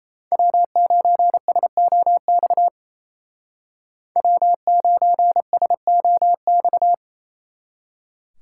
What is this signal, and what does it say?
Morse code, heard as a keyed tone, about 33 wpm, 700 Hz. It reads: W9SOX W9SOX
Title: Speed Racing - USA 1x3 Call Signs 28wpm